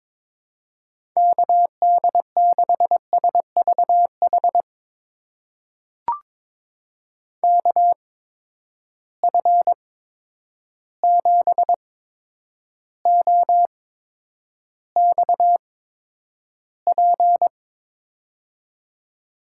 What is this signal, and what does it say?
Morse code, heard as a keyed tone, about 22 wpm, spaced as for 9 wpm, 700 Hz. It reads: KD6SVH KF7OXP
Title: USA Call Signs - 6 Characters - 8x Character Spacing - 22wpm